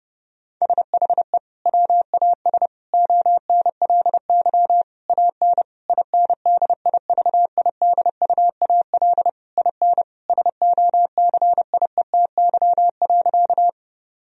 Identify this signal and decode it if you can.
Morse code, heard as a keyed tone, about 30 wpm, 700 Hz. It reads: SHE WAS ONLY AN INDIVIDUAL IN SOCIETY.
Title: Sentences from Top 1000 Words - Review Part 4 - 30wpm